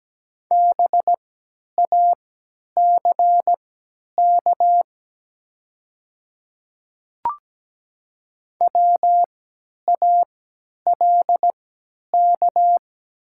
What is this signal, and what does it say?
Morse code, heard as a keyed tone, about 17 wpm, spaced as for 10 wpm, 700 Hz. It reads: BACK WALK
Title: 3x Character Spacing - Top 300 Words - 17wpm